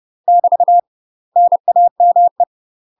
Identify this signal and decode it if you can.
Morse code, heard as a keyed tone, about 30 wpm, 700 Hz. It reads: <BT> NAME
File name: UK MASTER QSOs TRAINING / 30WPM / UK MASTER QSOs 0039